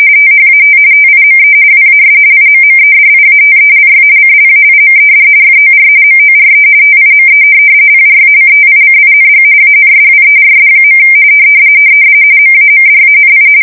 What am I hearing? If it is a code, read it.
RTTY, about 45.45 baud. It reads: THAT TIME WAS IMMEDIATELY CLAIMED FOR MORE TIME PRACTICING ANSWERS. MAMDANI
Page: 170 Shift 2125/2295